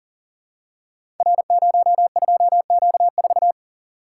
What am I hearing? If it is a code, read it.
Morse code, heard as a keyed tone, about 39 wpm, 690 Hz. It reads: R02Q4
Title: Code Groups - 5 Letter:Numbers - 40wpm